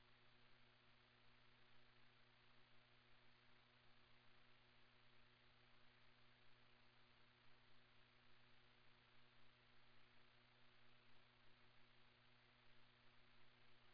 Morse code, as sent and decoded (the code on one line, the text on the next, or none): none